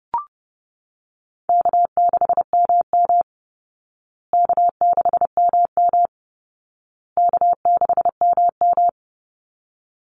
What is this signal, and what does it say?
Morse code, heard as a keyed tone, about 30 wpm, 700 Hz. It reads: K6MM K6MM K6MM
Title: Call Signs Repeated 2x - 30wpm